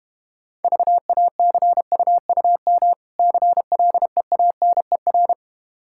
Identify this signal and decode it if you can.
Morse code, heard as a keyed tone, about 32 wpm, 700 Hz. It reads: VACUUM CLEANER